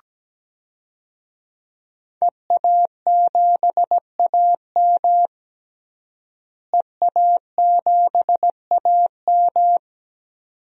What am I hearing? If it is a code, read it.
Morse code, heard as a keyed tone, about 17 wpm, 700 Hz. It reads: EA7AM EA7AM
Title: DXCC-100 Call Signs - 5 Characters - Repeated 2x - 17wpm